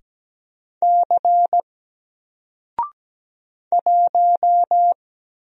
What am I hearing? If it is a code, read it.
Morse code, heard as a keyed tone, about 17 wpm, 700 Hz. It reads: C 1
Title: Single Letter-Number 17wpm